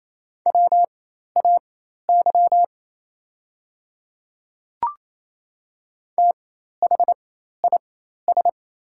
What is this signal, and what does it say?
Morse code, heard as a keyed tone, about 28 wpm, spaced as for 13 wpm, 700 Hz. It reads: WAY THIS
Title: 4x Character Spacing - Top 100 Words - 28wpm